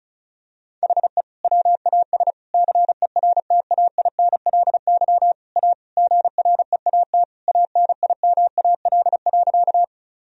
Morse code, heard as a keyed tone, about 36 wpm, 700 Hz. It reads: HE WAS CERTAINLY A GREAT ANIMAL.
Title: Sentences from Top 1000 Words - Review Part 1 - 35wpm